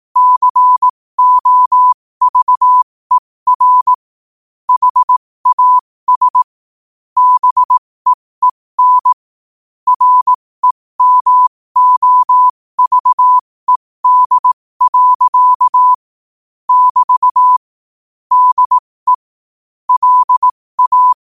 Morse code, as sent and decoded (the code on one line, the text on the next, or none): -.-. --- ...- . .-. / .... .- ... / -... . -. / .-. -- --- ...- . -.. .-.-.- / -...- / -.. / .-.. .-
COVER HAS BEEN REMOVED. <BT> DE LA